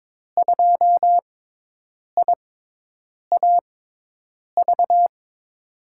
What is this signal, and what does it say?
Morse code, heard as a keyed tone, about 21 wpm, spaced as for 8 wpm, 700 Hz. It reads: 2IAV
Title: USA Call Signs - 5 Characters - 6x Character Spacing - 22wpm